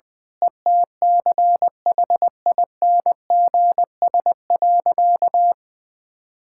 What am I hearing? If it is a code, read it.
Morse code, heard as a keyed tone, about 21 wpm, 700 Hz. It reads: ETCHINGS.
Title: Pangram Challenge v2 - 20wpm